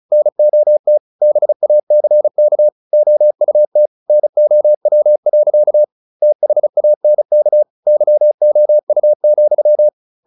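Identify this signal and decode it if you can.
Morse code, heard as a keyed tone, about 34 wpm, 600 Hz. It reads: NOT BACK OUT NOW. THANK YOU,
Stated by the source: Morse practice at 35 wpm from CW med Gnister, sending Ken Follett's novel Never in English.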